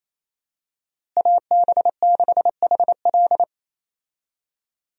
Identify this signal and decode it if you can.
Morse code, heard as a keyed tone, about 27 wpm, 700 Hz. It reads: AB6HL
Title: USA Call Signs - 5 Characters - Repeated 3x - 28wpm